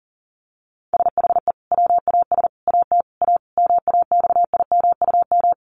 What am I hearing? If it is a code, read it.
Morse code, heard as a keyed tone, about 40 wpm, 700 Hz. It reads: SHE WAS AT A MAXIMUM
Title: Sentences from Top 1300 Words 40wpm